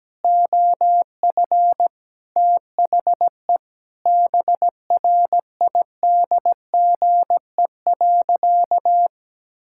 Morse code, heard as a keyed tone, about 17 wpm, 700 Hz. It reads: OF THE BRIDGE.
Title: Sentences from Top 1600 Words 17wpm